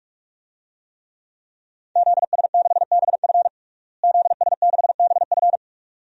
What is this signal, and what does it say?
Morse code, heard as a keyed tone, about 46 wpm, 700 Hz. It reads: ZS6BF ZS6BF
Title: DXCC-100 Call Signs - 5 Characters - Repeated 2x - 45wpm